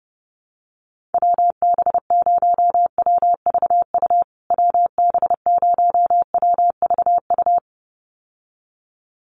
Morse code, heard as a keyed tone, about 30 wpm, 700 Hz. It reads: WB0WVU WB0WVU
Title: USA Call Signs Repeated 2x 30wpm